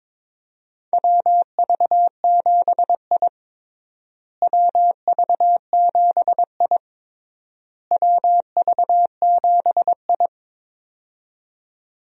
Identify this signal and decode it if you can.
Morse code, heard as a keyed tone, about 22 wpm, 700 Hz. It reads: WV7I WV7I WV7I
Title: USA Call Signs - 4 Characters - Repeated 3x - 22wpm